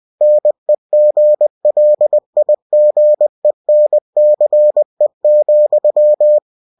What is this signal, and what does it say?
Morse code, heard as a keyed tone, about 20 wpm, 600 Hz. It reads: NEGLIGENCE,